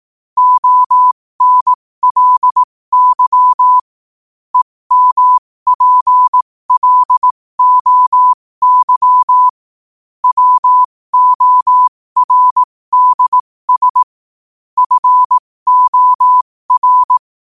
Morse code, heard as a keyed tone, about 18 wpm, spaced as for 15 wpm, 1000 Hz. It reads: ONLY EMPLOY WORDS FOR
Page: New quotes every day in morse code at 15 Words per minute.